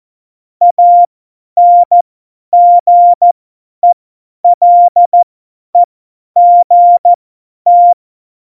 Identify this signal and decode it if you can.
Morse code, heard as a keyed tone, about 14 wpm, spaced as for 10 wpm, 700 Hz. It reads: ANGELEGT
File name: Morsekurs / Witze / Witz 17. November 2020, 50 F70